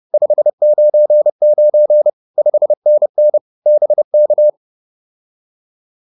Morse code, heard as a keyed tone, about 30 wpm, 600 Hz. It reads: 599 5NN BK